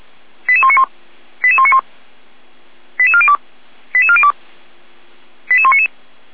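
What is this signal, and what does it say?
Five-tone selective call (ZVEI), heard as quick runs of five tones, tones of about 70 ms. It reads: 88191 88191 88402 88402 88199
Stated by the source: ZVEI five-tone selective call received on SDR.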